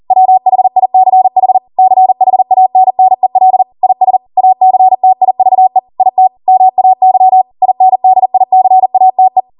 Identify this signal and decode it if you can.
Morse code, heard as a keyed tone, about 40 wpm, 750 Hz. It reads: WHICH CHANNEL IS ACTIVE IT MAY INDICATE